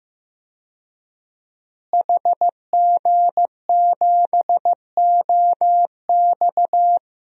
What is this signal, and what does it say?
Morse code, heard as a keyed tone, about 15 wpm, 700 Hz. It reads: HG7OX
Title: DXCC-100 Call Signs - 5 Characters - Repeated 2x - 15wpm